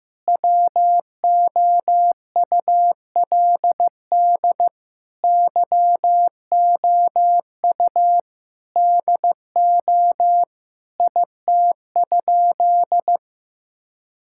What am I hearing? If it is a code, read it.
Morse code, heard as a keyed tone, about 15 wpm, 700 Hz. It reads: WOULD YOU DO IT?
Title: Sentences from Top 1000 Words - Review Part 5 - 15wpm